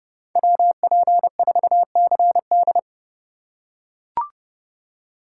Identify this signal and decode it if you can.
Morse code, heard as a keyed tone, about 29 wpm, 700 Hz. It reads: WP4CD